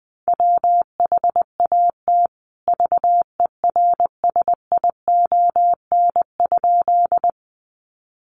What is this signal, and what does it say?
Morse code, heard as a keyed tone, about 20 wpm, 700 Hz. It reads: WHAT VERSION?
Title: Sentences from Top 1400 Words 20wpm